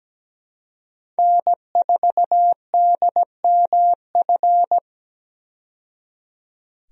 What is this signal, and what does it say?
Morse code, heard as a keyed tone, about 17 wpm, 700 Hz. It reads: N4DMF
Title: USA Call Signs - 5 Characters - 17wpm